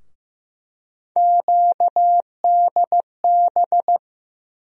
Morse code, heard as a keyed tone, about 16 wpm, 700 Hz. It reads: QDB
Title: USA Call Sign Postfixes - 3 Letters 15wpm